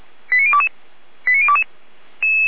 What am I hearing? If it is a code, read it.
Five-tone selective call (ZVEI), heard as quick runs of five tones, tones of about 80 ms. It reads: 89022 89022